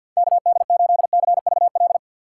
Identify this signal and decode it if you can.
Morse code, heard as a keyed tone, about 63 wpm, 700 Hz. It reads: KD8CVL